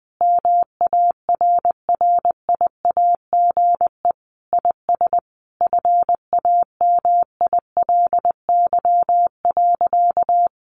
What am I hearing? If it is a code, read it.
Morse code, heard as a keyed tone, about 19 wpm, 700 Hz. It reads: MARRIAGE IS FAMILY.
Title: Sentences from Top 800 Words 20wpm